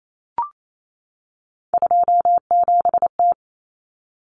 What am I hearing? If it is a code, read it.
Morse code, heard as a keyed tone, about 28 wpm, 700 Hz. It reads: 27T
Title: Common Cut Numbers - 3-Digit 28wpm